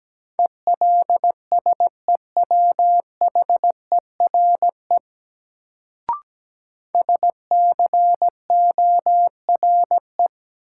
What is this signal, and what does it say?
Morse code, heard as a keyed tone, about 17 wpm, 700 Hz. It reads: ELSEWHERE SCORE
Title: Sentences from Top 1600 Words - New Word Drill - 17wpm